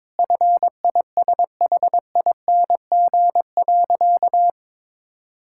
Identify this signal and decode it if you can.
Morse code, heard as a keyed tone, about 22 wpm, 700 Hz. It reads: FISHING.
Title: Sentences from Top 1000 Words - Review Part 5 - 22wpm